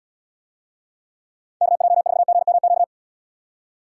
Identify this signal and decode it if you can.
Morse code, heard as a keyed tone, about 78 wpm, 700 Hz. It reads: KJ4GMQ